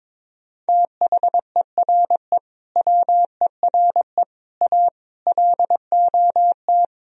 Morse code, heard as a keyed tone, about 23 wpm, 700 Hz. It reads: THERE WERE A LOT
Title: Speed-Racing - Sets of 5 Words 22wpm